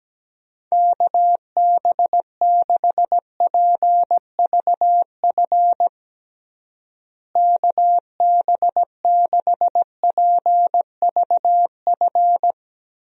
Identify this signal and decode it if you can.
Morse code, heard as a keyed tone, about 17 wpm, 700 Hz. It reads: KB6PVF KB6PVF
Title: USA Call Signs - 6 Characters - Repeated 2x - 17wpm